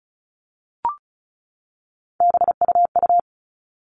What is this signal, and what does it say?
Morse code, heard as a keyed tone, about 34 wpm, 700 Hz. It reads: BUU
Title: All Cut Numbers - 3-Digit 35wpm